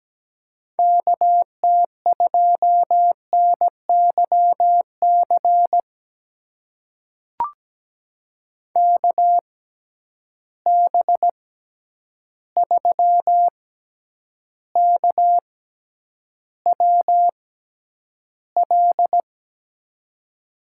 Morse code, heard as a keyed tone, about 17 wpm, spaced as for 9 wpm, 700 Hz. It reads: KT2NYC KB3KWL
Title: USA Call Signs - 6 Characters - 6x Character Spacing - 17wpm